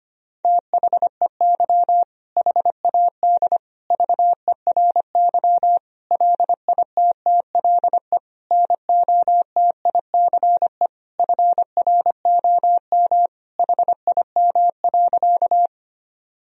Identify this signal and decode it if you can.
Morse code, heard as a keyed tone, about 25 wpm, 700 Hz. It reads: THEY HAD VERY LITTLE NOTICE FROM HIM.
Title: Sentences from Top 1000 Words - Review Part 2 - 25wpm